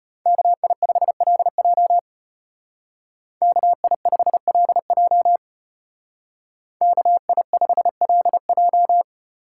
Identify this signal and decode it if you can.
Morse code, heard as a keyed tone, about 33 wpm, 700 Hz. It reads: KI5LJ KI5LJ KI5LJ